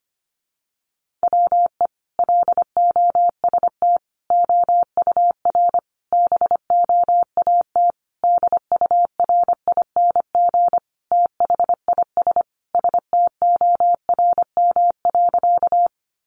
Morse code, heard as a keyed tone, about 25 wpm, 700 Hz. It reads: WE LOST OUR BOAT DURING THIS STORM.
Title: Sentences from Top 1500 Words 25wpm